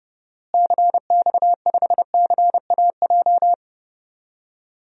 Morse code, heard as a keyed tone, about 30 wpm, 700 Hz. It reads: CX5CAJ
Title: DXCC-100 Call Signs - 6 Characters - Repeated 2x - 30wpm